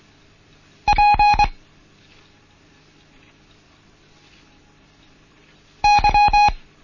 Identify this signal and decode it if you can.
Morse code, heard as a keyed tone, about 24 wpm, 830 Hz. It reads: P Y
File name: Morsen / Kurs A 16